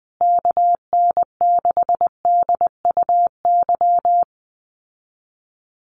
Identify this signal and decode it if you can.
Morse code, heard as a keyed tone, about 20 wpm, 700 Hz. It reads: KN6DUY
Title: USA Call Signs Repeated 3x 20wpm